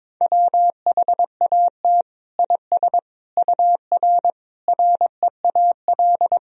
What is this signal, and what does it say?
Morse code, heard as a keyed tone, about 22 wpm, 700 Hz. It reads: WHAT IS UR REAL
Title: Sentences from Top 300 Words and Abbreviations 22wpm